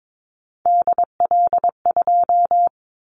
Morse code, heard as a keyed tone, about 22 wpm, 700 Hz. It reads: DL2
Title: Callsign Prefixes for Top 25 DXCC - 22wpm